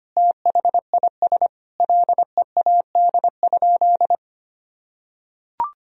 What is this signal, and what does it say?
Morse code, heard as a keyed tone, about 25 wpm, 700 Hz. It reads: THIS LEAD?
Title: Sentences from Top 1000 Words - Review Part 2 - 25wpm